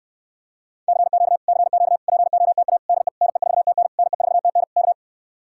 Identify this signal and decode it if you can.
Morse code, heard as a keyed tone, about 69 wpm, 700 Hz. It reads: CQ CQ POTA DE NE4TN NE4TN K